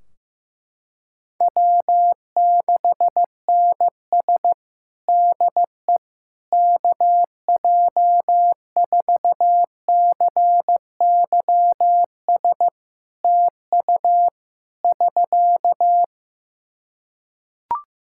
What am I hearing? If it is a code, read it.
Morse code, heard as a keyed tone, about 15 wpm, 700 Hz. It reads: W6NS DE KJ4CYS TU <SK>
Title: Practice QSO - Exchange 4 - Ending - 15wpm